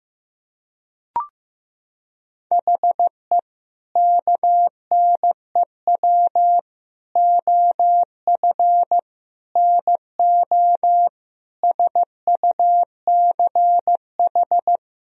Morse code, heard as a keyed tone, about 15 wpm, 700 Hz. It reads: HE KNEW OF NO SUCH
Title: Sentences from Top 1000 Words - Review Part 2 - 15wpm